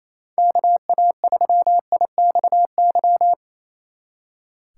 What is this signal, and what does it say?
Morse code, heard as a keyed tone, about 36 wpm, 700 Hz. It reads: KA3IXY